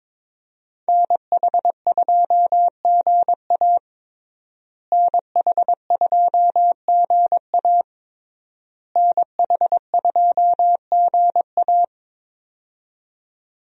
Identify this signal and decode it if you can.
Morse code, heard as a keyed tone, about 22 wpm, 700 Hz. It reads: NH2GA NH2GA NH2GA
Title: USA Call Signs - 5 Characters - Repeated 3x - 22wpm